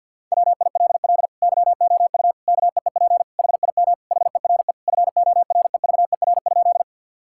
Morse code, heard as a keyed tone, about 52 wpm, 700 Hz. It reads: WILL YOU KEEP HIM HERE FOREVER?
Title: Sentences from Top 1600 Words 50wpm